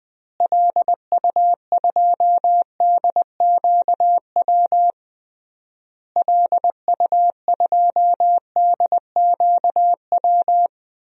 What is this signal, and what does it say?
Morse code, heard as a keyed tone, about 20 wpm, 700 Hz. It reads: LU2DQW LU2DQW
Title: DXCC-100 Call Signs - 6 Characters - Repeated 3x - 20wpm